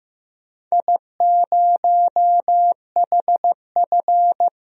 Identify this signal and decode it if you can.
Morse code, heard as a keyed tone, about 15 wpm, 700 Hz. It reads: I0HF
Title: DXCC-100 Call Signs - 4 Characters - 15wpm